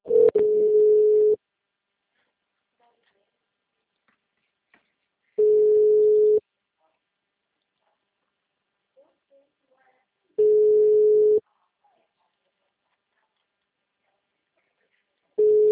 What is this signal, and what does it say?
Ringback tone (Europe, 425 Hz), with speech.
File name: out- -2010- - - .310